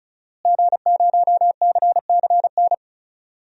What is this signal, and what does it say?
Morse code, heard as a keyed tone, about 36 wpm, 700 Hz. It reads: G0CCN
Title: DXCC-100 Call Signs - 5 Characters - Repeated 3x - 35wpm